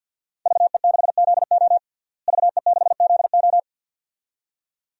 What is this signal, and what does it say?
Morse code, heard as a keyed tone, about 52 wpm, 700 Hz. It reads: VE6ZO VE6ZO
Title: DXCC-100 Call Signs - 5 Characters - Repeated 2x - 50wpm